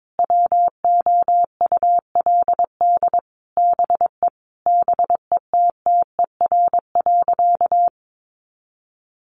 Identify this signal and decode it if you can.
Morse code, heard as a keyed tone, about 22 wpm, 700 Hz. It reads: WOULD BE BETTER.
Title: Sentences from Top 400 Words 22wpm